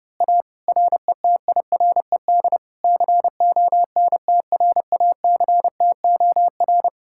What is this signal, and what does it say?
Morse code, heard as a keyed tone, about 28 wpm, 700 Hz. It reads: A RETIRED CONTRACTOR
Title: Speed-Racing Common QSO Phrases Encore - 30wpm